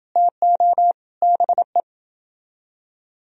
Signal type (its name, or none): Morse code